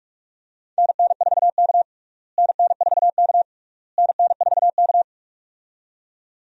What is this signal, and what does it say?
Morse code, heard as a keyed tone, about 41 wpm, 700 Hz. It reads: NN4K NN4K NN4K
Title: USA Call Signs - 4 Characters - Repeated 3x - 45wpm